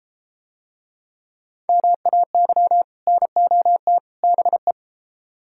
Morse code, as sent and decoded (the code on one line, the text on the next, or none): -- .- -.-- / -. --- - / -... .
MAY NOT BE